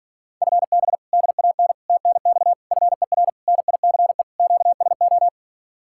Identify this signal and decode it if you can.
Morse code, heard as a keyed tone, about 52 wpm, 700 Hz. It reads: FB DAN TNX FER NICE QSO